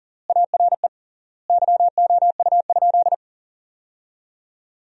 Morse code, heard as a keyed tone, about 37 wpm, 700 Hz. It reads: ARE YOU?
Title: 3x Word-Spacing - Sentences from Top 100 Words - 40wpm